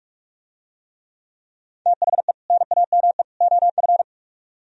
Morse code, heard as a keyed tone, about 43 wpm, 700 Hz. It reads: THE NAME OF